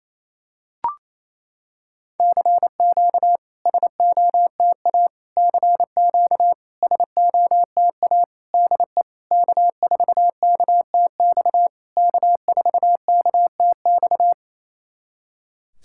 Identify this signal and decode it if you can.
Morse code, heard as a keyed tone, about 28 wpm, 700 Hz. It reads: CQ SOTA CQ SOTA DE K4KTX K4KTX
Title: SOTA QSO Practice 28wpm